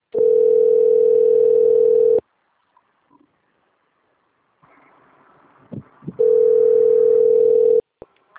Ringback tone (North America), 440 + 480 Hz.
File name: rg-1600-unknown- - - .4329